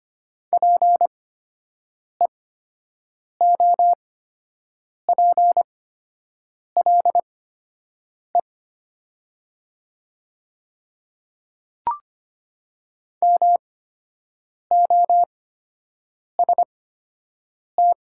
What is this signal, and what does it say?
Morse code, heard as a keyed tone, about 25 wpm, spaced as for 7 wpm, 700 Hz. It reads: PEOPLE MOST
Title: 8x Character Spacing - Top 100 Words - 25wpm